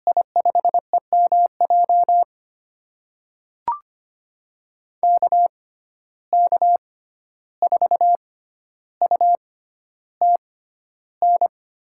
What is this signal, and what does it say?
Morse code, heard as a keyed tone, about 25 wpm, spaced as for 12 wpm, 700 Hz. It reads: I5EMJ KK4UTN
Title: USA Call Signs - 6 Characters - 6x Character Spacing - 25wpm